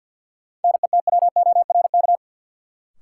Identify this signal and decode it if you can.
Morse code, heard as a keyed tone, about 50 wpm, 700 Hz. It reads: NETWORK
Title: Sentences from Top 1000 Words - Review Part 7 - New Word Drill - 50wpm